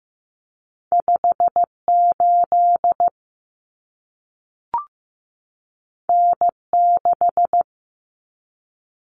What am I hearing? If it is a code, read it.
Morse code, heard as a keyed tone, about 15 wpm, 700 Hz. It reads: 58 N6
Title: Common Cut Numbers - 2-Digit 15wpm